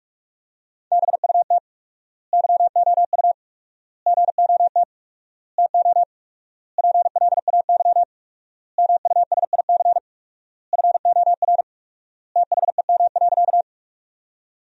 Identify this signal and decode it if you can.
Morse code, heard as a keyed tone, about 46 wpm, 700 Hz. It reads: BUT YOU GOT TO PLAY MUSIC FOR THEM.
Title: 4x Word-Spacing - Sentences from Top 300 Words - 45wpm